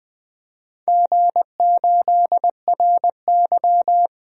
Morse code, heard as a keyed tone, about 21 wpm, 700 Hz. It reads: G8RY
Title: DXCC-100 Call Signs - 4 Characters - Repeated 3x - 20wpm